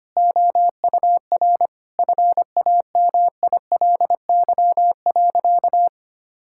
Morse code, heard as a keyed tone, about 25 wpm, 700 Hz. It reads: OUR FAMILY.
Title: Sentences from Top 1000 Words - Review Part 6 - 25wpm